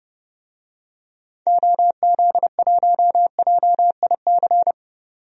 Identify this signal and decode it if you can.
Morse code, heard as a keyed tone, about 30 wpm, 700 Hz. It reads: OZ1JIC
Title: DXCC-100 Call Signs - 6 Characters - 30wpm